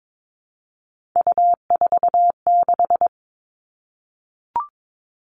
Morse code, heard as a keyed tone, about 22 wpm, 700 Hz. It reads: U46
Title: Three Challenging Letter-Number Combinations 22wpm